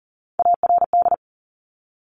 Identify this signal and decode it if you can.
Morse code, heard as a keyed tone, about 41 wpm, 700 Hz. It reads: ARD
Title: Top Three Letter Combinations in Words 40wpm